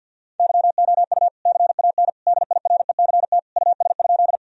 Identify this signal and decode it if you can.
Morse code, heard as a keyed tone, about 50 wpm, 700 Hz. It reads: YOU CAN DIRECT US?
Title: Sentences from Top 1000 Words - Review Part 2 - 50wpm